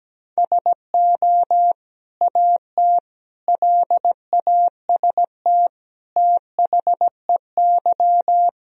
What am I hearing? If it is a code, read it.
Morse code, heard as a keyed tone, about 17 wpm, 700 Hz. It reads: SO AT LAST THEY
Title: Sentences from Top 1000 Words - Review Part 1 - 17wpm